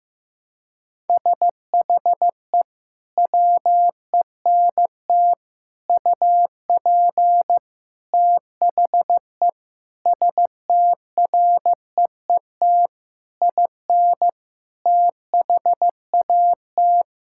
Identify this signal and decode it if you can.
Morse code, heard as a keyed tone, about 16 wpm, 700 Hz. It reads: SHE WENT UP THE STREET IN THAT
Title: Sentences from Top 1000 Words - Review Part 5 - 15wpm